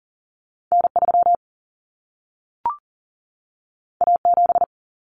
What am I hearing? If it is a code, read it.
Morse code, heard as a keyed tone, about 40 wpm, 700 Hz. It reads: N3 A7
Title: Common Cut Numbers - 2-Digit 40wpm